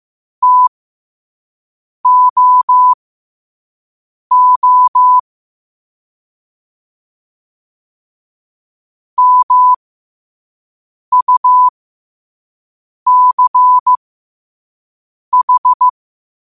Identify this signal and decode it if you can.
Morse code, heard as a keyed tone, about 15 wpm, spaced as for 5 wpm, 1000 Hz. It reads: TOO MUCH
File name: qotdfiles / 5wpm